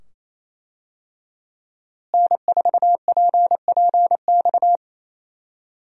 Morse code, heard as a keyed tone, about 27 wpm, 700 Hz. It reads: N4PPX
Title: USA Call Signs - 5 Characters - Repeated 2x - 28wpm